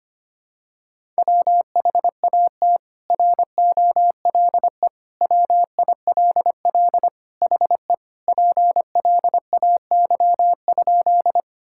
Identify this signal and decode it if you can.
Morse code, heard as a keyed tone, about 25 wpm, 700 Hz. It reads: WHAT ROLE WILL HE PLAY?
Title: Sentences from Top 1000 Words - Review Part 6 - 25wpm